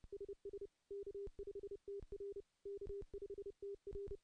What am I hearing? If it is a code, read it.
Morse code, heard as a keyed tone, about 30 wpm, 400 Hz. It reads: SS K5TR K5TR
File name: contests / CW-SS-k5tr- -0600